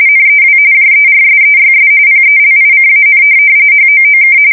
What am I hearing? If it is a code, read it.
RTTY, about 45.45 baud. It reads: NEW JERSEYS PELICAN HOOK
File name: fsk170 / USS-New-Jersey